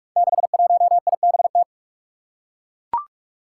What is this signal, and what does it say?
Morse code, heard as a keyed tone, about 41 wpm, 700 Hz. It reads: 61IBT
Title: Five Challenging Letter-Number Combinations 45wpm